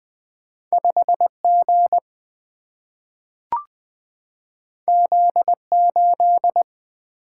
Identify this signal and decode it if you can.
Morse code, heard as a keyed tone, about 20 wpm, 700 Hz. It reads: 5G Z8
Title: Single-Letter Single-Number 20wpm